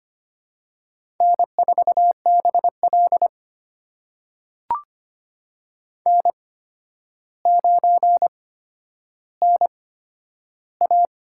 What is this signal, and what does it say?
Morse code, heard as a keyed tone, about 25 wpm, spaced as for 11 wpm, 700 Hz. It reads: N4BL N9NA
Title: USA Call Signs - 4 Characters - 8x Character Spacing - 25wpm